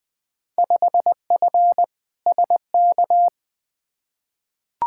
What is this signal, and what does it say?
Morse code, heard as a keyed tone, about 20 wpm, 700 Hz. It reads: 5F SK
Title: Field Day Exchanges Encore - 20wpm